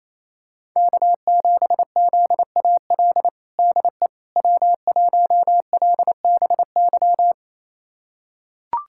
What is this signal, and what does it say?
Morse code, heard as a keyed tone, about 28 wpm, 700 Hz. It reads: K7ZAL DE W1LBY
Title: USA Callsign DE Callsign - 5 Characters - 28wpm